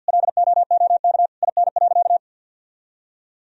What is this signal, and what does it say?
Morse code, heard as a keyed tone, about 52 wpm, 700 Hz. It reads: LOOK IN.
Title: Sentences from Top 1000 Words - Review Part 2 - 50wpm